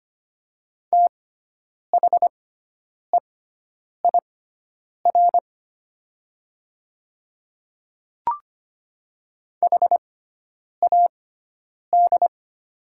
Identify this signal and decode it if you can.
Morse code, heard as a keyed tone, about 25 wpm, spaced as for 9 wpm, 700 Hz. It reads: THEIR HAD
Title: 6x Character Spacing - Top 200 Words - 25wpm